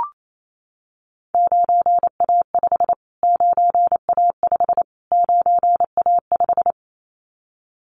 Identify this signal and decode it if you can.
Morse code, heard as a keyed tone, about 28 wpm, 700 Hz. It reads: 9A5 9A5 9A5
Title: Callsign Prefixes for Top 25 DXCC - Repeated 3x - 28wpm